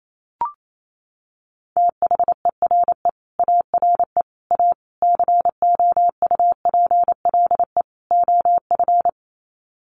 Morse code, heard as a keyed tone, about 28 wpm, 700 Hz. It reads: THERE ARE A COUPLE OF
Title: Sets of 5 Words Encore 28wpm